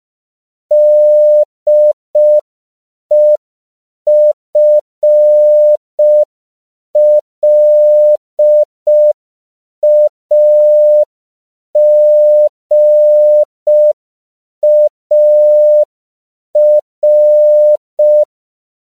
Morse code, heard as a keyed tone, about 5 wpm, 600 Hz. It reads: DEFLAGAR
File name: CW / 5PPM